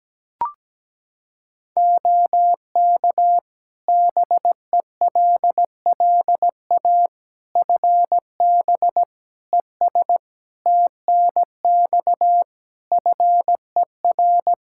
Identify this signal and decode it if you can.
Morse code, heard as a keyed tone, about 17 wpm, 700 Hz. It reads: OK BELLA FB ES TNX FER
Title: K1USN SST QSO Practice 17wpm